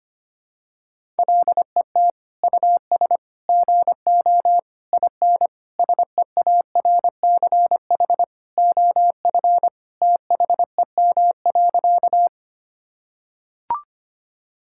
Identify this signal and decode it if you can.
Morse code, heard as a keyed tone, about 25 wpm, 700 Hz. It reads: LET US GO IN SEARCH OF THEM.
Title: Sentences from Top 1000 Words - Review Part 5 - 25wpm